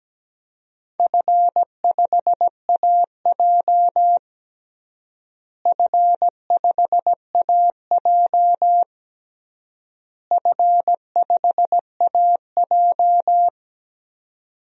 Morse code, heard as a keyed tone, about 17 wpm, 700 Hz. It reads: F5AJ F5AJ F5AJ
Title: DXCC-100 Call Signs - 4 Characters - Repeated 3x - 17wpm